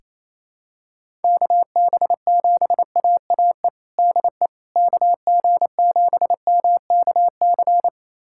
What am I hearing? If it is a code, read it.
Morse code, heard as a keyed tone, about 28 wpm, 700 Hz. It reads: KB7AAE DE KG7MKC
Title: USA Callsign DE Callsign - 6 Characters - 28wpm